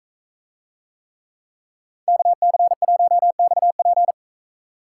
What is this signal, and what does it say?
Morse code, heard as a keyed tone, about 40 wpm, 700 Hz. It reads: KC1XP